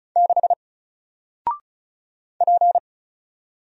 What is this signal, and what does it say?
Morse code, heard as a keyed tone, about 36 wpm, 700 Hz. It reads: 6 P
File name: Single-Letter-Number-35wpm